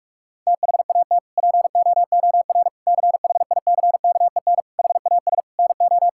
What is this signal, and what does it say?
Morse code, heard as a keyed tone, about 46 wpm, 700 Hz. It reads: THAT POOR CHICKEN HAS NO
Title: Sentences from Top 1000 Words - Review Part 5 - 45wpm